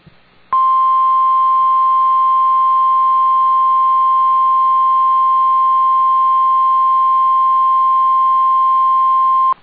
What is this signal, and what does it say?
Emergency Alert System (SAME) tones (the 1050 Hz attention tone). It is heard from 0.5 to 9.1 s.